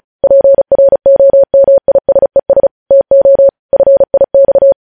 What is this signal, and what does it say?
Morse code, heard as a keyed tone, about 36 wpm, 550 Hz. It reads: PROMISES TO FIX
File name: podcast / News headlines-1x-35x35